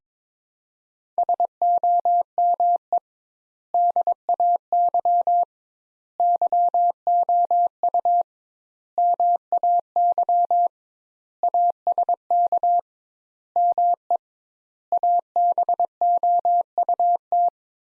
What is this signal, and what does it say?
Morse code, heard as a keyed tone, about 22 wpm, 700 Hz. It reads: SOME DAY YOU MAY ASK ME ABOUT
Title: Sentences Top 200 Words - 22wpm